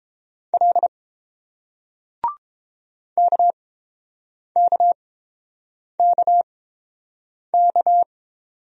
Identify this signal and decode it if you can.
Morse code, heard as a keyed tone, about 29 wpm, spaced as for 7 wpm, 700 Hz. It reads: L KKKK